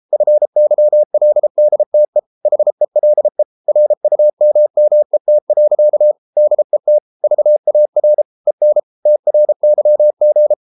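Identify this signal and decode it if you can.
Morse code, heard as a keyed tone, about 33 wpm, 600 Hz. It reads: FYLDTE HELE RUMMET. DET VAR EN TRYG
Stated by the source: Morse code at 33 wpm, QRQ.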